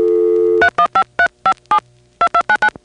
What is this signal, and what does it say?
DTMF, heard as telephone touch tones, tones of about 80 ms. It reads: 315357 2299